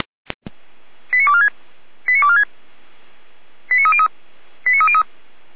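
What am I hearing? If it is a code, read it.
Five-tone selective call (ZVEI), heard as quick runs of five tones, tones of about 80 ms. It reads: 89246 89246 89303 89303
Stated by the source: ZVEI five-tone selective call received on an SDR.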